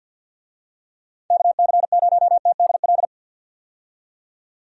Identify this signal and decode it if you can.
Morse code, heard as a keyed tone, about 50 wpm, 700 Hz. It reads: KC0TDL
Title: USA Call Signs - 6 Characters - 50wpm